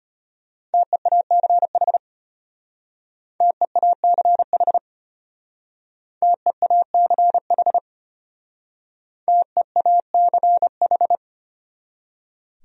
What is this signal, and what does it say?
Morse code, heard as a keyed tone, about 33 wpm, 700 Hz. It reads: TEACH TEACH TEACH TEACH